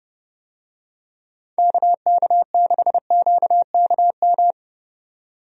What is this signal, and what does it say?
Morse code, heard as a keyed tone, about 30 wpm, 700 Hz. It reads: KK6QKM
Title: USA Call Signs - 6 Characters - 30wpm